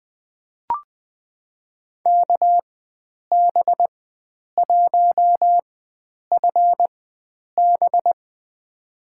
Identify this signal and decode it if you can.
Morse code, heard as a keyed tone, about 20 wpm, spaced as for 9 wpm, 700 Hz. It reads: KB1FB
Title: USA Call Signs - 5 Characters - 4x Character Spacing - 20wpm